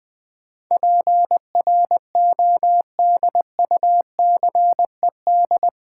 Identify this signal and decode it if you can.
Morse code, heard as a keyed tone, about 20 wpm, 700 Hz. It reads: PRODUCED
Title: Sentences from Top 1000 Words - Review Part 6 - New Word Drill - 20wpm